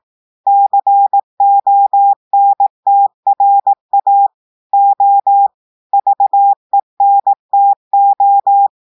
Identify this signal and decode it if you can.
Morse code, heard as a keyed tone, about 18 wpm, 800 Hz. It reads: CONTRA O VENTO
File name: audios / portugais / portugais texte en clair 1 vitesse 18 mots minutes